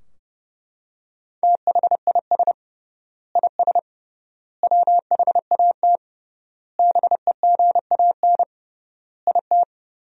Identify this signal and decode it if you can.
Morse code, heard as a keyed tone, about 30 wpm, 700 Hz. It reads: THIS IS WHAT BEGAN IT
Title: 3x Word-Spacing - Sentences from Top 300 Words - 30wpm